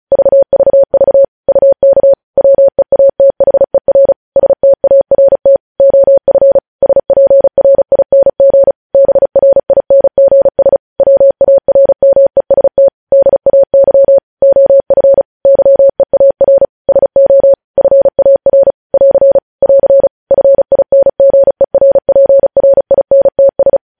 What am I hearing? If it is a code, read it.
Morse code, heard as a keyed tone, about 36 wpm, 550 Hz. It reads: VVV UK WEATHER START OF SPRING BRINGS WARMEST DAY OF YEAR SO FAR <AR> <AR> FINGERPRINTS
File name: podcast / News headlines-1x-35x35